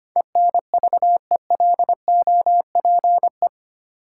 Morse code, heard as a keyed tone, about 25 wpm, 700 Hz. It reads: ENVELOPE
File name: practice / 25wpm / Easy Words with Hints - 25wpm